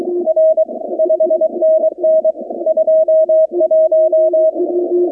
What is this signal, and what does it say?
Morse code, heard as a keyed tone, about 23 wpm, 640 Hz, with radio noise. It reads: R 5NN 21
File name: contests / 2019-cw-wpx-kt5j-k5pi- -2330 0-25